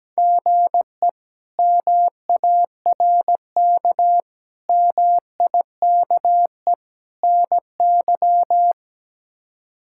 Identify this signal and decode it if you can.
Morse code, heard as a keyed tone, about 17 wpm, 700 Hz. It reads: GE MARK MIKE NY
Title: K1USN SST QSO Practice - 17wpm